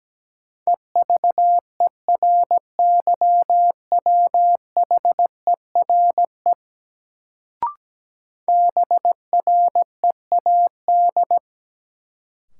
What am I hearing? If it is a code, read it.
Morse code, heard as a keyed tone, about 17 wpm, 700 Hz. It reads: EVERYWHERE BREAD
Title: Sentences from Top 1600 Words - New Word Drill Alt - 17wpm